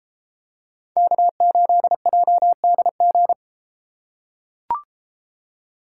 Morse code, heard as a keyed tone, about 33 wpm, 700 Hz. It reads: K8JDG